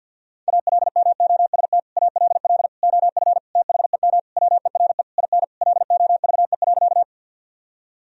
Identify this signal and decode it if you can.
Morse code, heard as a keyed tone, about 51 wpm, 700 Hz. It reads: ALMOST ALL OF THEM WERE IN LOVE.
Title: Sentences from Top 1000 Words - Review Part 4 - 50wpm